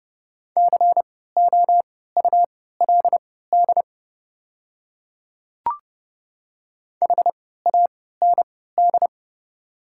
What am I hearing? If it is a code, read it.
Morse code, heard as a keyed tone, about 30 wpm, spaced as for 17 wpm, 690 Hz. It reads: COULD HAND
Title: 3x Character Spacing - Top 200 Words - 30wpm